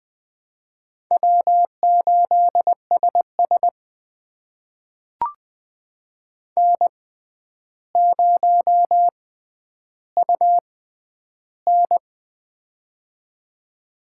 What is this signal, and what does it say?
Morse code, heard as a keyed tone, about 20 wpm, spaced as for 10 wpm, 700 Hz. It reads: W8SS N0UN
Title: USA Call Signs - 4 Characters - 6x Character Spacing - 20wpm